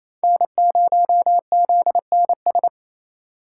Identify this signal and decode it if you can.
Morse code, heard as a keyed tone, about 28 wpm, 700 Hz. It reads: N0ZNS
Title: USA Call Signs - 5 Characters - 4x Character Spacing - 28wpm